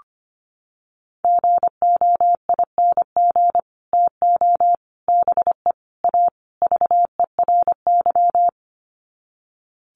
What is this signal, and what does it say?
Morse code, heard as a keyed tone, about 25 wpm, 700 Hz. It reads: GOING TO BE A VERY
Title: Sets of 5 Words Encore 25wpm